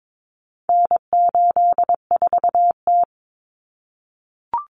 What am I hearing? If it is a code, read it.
Morse code, heard as a keyed tone, about 22 wpm, 700 Hz. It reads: N84T
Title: Common Cut Numbers - 4-Digit 22wpm